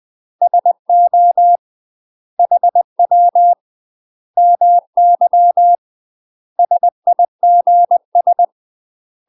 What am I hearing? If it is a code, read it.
Morse code, heard as a keyed tone, about 20 wpm, 700 Hz. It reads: SO HW MY SIGS
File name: UK MASTER QSOs TRAINING / 20WPM / UK MASTER QSOs 0029